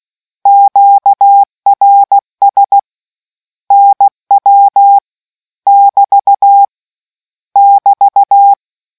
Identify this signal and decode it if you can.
Morse code, heard as a keyed tone, about 16 wpm, 800 Hz. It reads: QRS NW <BT> <BT>